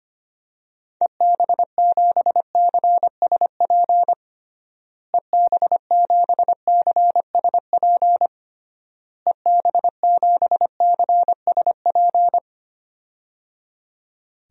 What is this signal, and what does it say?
Morse code, heard as a keyed tone, about 25 wpm, 700 Hz. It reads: EB7CSP EB7CSP EB7CSP
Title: DXCC-100 Call Signs - 6 Characters - Repeated 3x - 25wpm